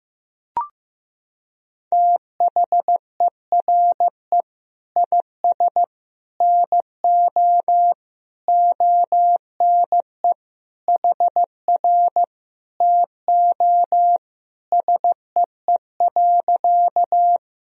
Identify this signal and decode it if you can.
Morse code, heard as a keyed tone, about 15 wpm, 700 Hz. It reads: THERE IS NO ONE HR TO SEE.
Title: Sentences from Top 200 Words and Abbreviations 15wpm